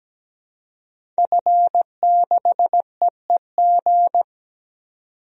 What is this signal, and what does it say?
Morse code, heard as a keyed tone, about 17 wpm, 700 Hz. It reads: F6EEG
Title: DXCC-100 Call Signs - 5 Characters - 17wpm